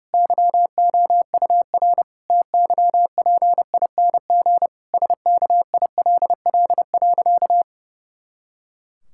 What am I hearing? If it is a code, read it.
Morse code, heard as a keyed tone, about 30 wpm, 700 Hz. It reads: YOUR TYPING SKILL.
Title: Pangram Challenge v2 - 30wpm